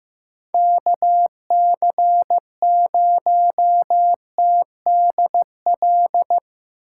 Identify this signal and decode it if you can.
Morse code, heard as a keyed tone, about 15 wpm, 700 Hz. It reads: KC0TDL
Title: USA Call Signs - 6 Characters - 4x Character Spacing - 15wpm